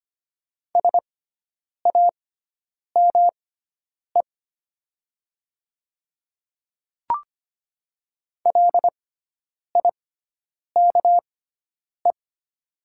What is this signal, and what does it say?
Morse code, heard as a keyed tone, about 25 wpm, spaced as for 9 wpm, 700 Hz. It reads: SAME LIKE
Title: 6x Character Spacing - Top 200 Words - 25wpm